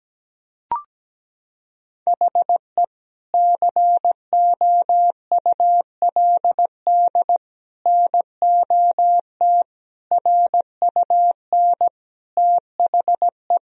Morse code, heard as a keyed tone, about 17 wpm, 700 Hz. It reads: HE COULD NOT RUN THE
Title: Sentences from Top 1000 Words - Review Part 2 - 17wpm